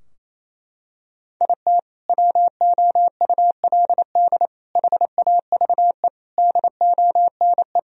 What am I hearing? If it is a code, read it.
Morse code, heard as a keyed tone, about 28 wpm, 700 Hz. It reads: IT WOULD HAVE DONE
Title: Sentences from Top 1000 Words - Review Part 2 - 28wpm